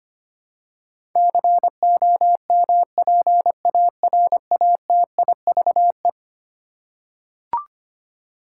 Morse code, heard as a keyed tone, about 25 wpm, 700 Hz. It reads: COMPARATIVE
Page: morse-code-ninja-practice-25wpm